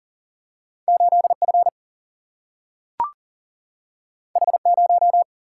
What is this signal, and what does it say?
Morse code, heard as a keyed tone, about 38 wpm, 700 Hz. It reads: 8 F H 0
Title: Single-Letter Single-Number 40wpm